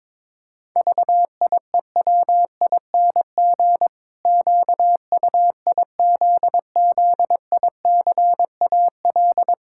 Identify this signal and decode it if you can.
Morse code, heard as a keyed tone, about 22 wpm, 700 Hz. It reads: VIEWING QUIZZICAL
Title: Pangram Challenge v2 - 22wpm